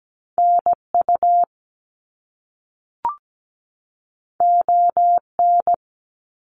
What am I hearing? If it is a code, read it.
Morse code, heard as a keyed tone, about 17 wpm, 700 Hz. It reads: NU ON
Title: Canadian Provinces and Territories 17wpm